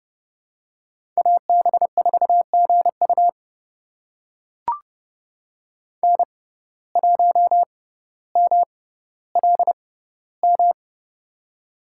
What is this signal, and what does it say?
Morse code, heard as a keyed tone, about 32 wpm, spaced as for 16 wpm, 700 Hz. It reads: AB4GU N1MLM
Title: USA Call Signs - 5 Characters - 6x Character Spacing - 30wpm